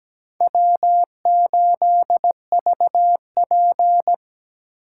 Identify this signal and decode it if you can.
Morse code, heard as a keyed tone, about 20 wpm, 700 Hz. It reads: W8VP
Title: Speed Racing - USA 1x2 Call Signs 17wpm